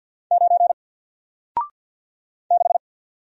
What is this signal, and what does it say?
Morse code, heard as a keyed tone, about 48 wpm, 700 Hz. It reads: 9 6
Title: Single Character (Letter:Number:Punctuation:Prosign) Alt 50wpm